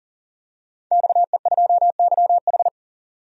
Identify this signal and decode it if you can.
Morse code, heard as a keyed tone, about 40 wpm, 700 Hz. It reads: XE2YH
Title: DXCC-100 Call Signs - 5 Characters - 40wpm